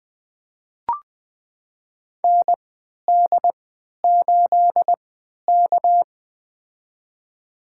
Morse code, heard as a keyed tone, about 20 wpm, spaced as for 11 wpm, 700 Hz. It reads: ND8K
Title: USA Call Signs - 4 Characters - 3x Character Spacing - 20wpm